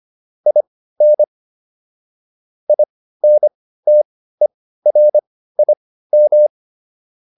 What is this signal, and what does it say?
Morse code, heard as a keyed tone, about 25 wpm, spaced as for 15 wpm, 600 Hz. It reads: IN INTERIM